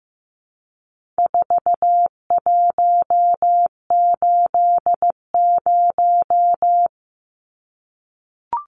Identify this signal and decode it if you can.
Morse code, heard as a keyed tone, about 15 wpm, 700 Hz. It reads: 4180
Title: Numbers - 4-Digit 15wpm